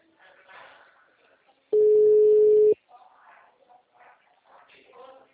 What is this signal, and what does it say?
Ringback tone (Europe, 425 Hz).